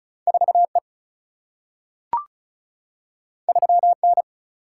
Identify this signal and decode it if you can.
Morse code, heard as a keyed tone, about 36 wpm, 700 Hz. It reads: 4E 3N
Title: Single-Letter Single-Number 35wpm